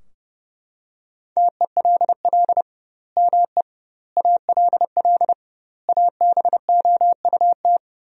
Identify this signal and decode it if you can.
Morse code, heard as a keyed tone, about 30 wpm, 700 Hz. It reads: TELL ME ALL ABOUT
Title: 2x Word-Spacing - Sentences from Top 200 Words - 30wpm